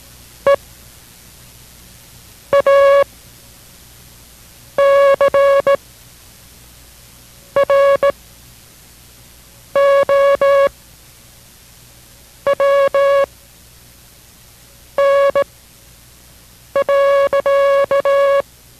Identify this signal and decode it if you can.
Morse code, heard as a keyed tone, about 18 wpm, spaced as for 5 wpm, 560 Hz. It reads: E A CROWN.